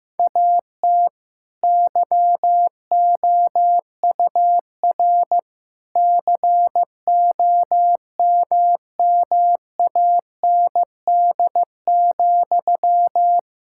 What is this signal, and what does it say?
Morse code, heard as a keyed tone, about 16 wpm, 700 Hz. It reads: AT YOUR COMMAND,
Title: Sentences from Top 1600 Words 15wpm